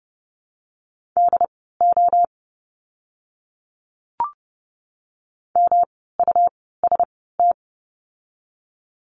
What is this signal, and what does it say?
Morse code, heard as a keyed tone, about 30 wpm, spaced as for 17 wpm, 700 Hz. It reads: DO MUST
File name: course / 30:10wpm / 237 Lesson 38f - Letter q - Words 30wpm